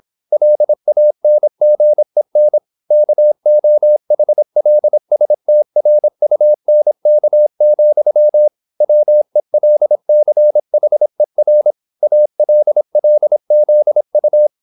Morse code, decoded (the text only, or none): LANGEN KOHLSTRUNK, WELCHER ALLZU